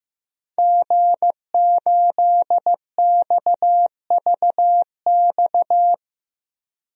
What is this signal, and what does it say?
Morse code, heard as a keyed tone, about 15 wpm, 700 Hz. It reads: G8XVX
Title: DXCC-100 Call Signs - 5 Characters - Repeated 2x - 15wpm